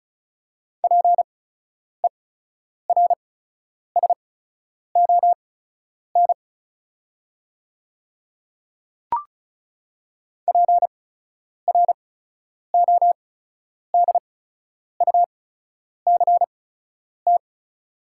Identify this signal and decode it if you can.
Morse code, heard as a keyed tone, about 36 wpm, spaced as for 10 wpm, 700 Hz. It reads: PERSON PRODUCT
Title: 8x Character Spacing - Top 500 Words - 35wpm